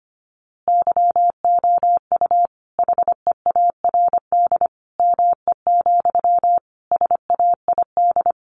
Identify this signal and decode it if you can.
Morse code, heard as a keyed tone, about 25 wpm, 700 Hz. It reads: YOU HEARD ME, SAID
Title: A Sound Of Thunder 25wpm